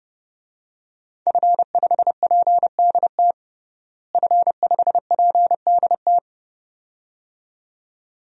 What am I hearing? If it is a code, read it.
Morse code, heard as a keyed tone, about 32 wpm, 700 Hz. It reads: F5PDT F5PDT
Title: DXCC-100 Call Signs - 5 Characters - Repeated 2x - 30wpm